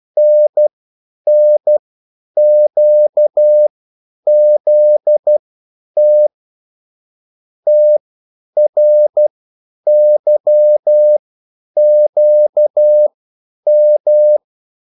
Morse code, heard as a keyed tone, about 12 wpm, spaced as for 9 wpm, 600 Hz. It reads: NNQZT TRYQM
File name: QSL / CW lesson